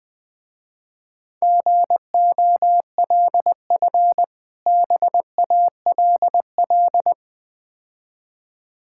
Morse code, heard as a keyed tone, about 21 wpm, 700 Hz. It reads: GOLF BALL
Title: Difficult Words with Hints - 20wpm